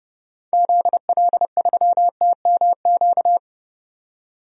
Morse code, heard as a keyed tone, about 30 wpm, 700 Hz. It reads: ZL3TMQ
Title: DXCC-100 Call Signs - 6 Characters - 30wpm